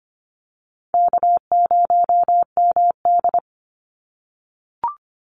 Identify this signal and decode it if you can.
Morse code, heard as a keyed tone, about 25 wpm, 700 Hz. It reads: K0MD
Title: Easy Call Signs - 25wpm